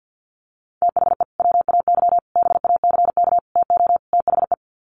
Morse code, heard as a keyed tone, about 48 wpm, 700 Hz. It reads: THE WAY BACK TO THE
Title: Sets of 5 Words Encore2 50wpm